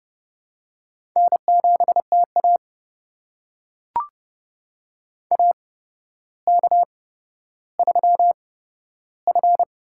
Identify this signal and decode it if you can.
Morse code, heard as a keyed tone, about 30 wpm, spaced as for 13 wpm, 700 Hz. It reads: N7TA AK3F
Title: USA Call Signs - 4 Characters - 8x Character Spacing - 30wpm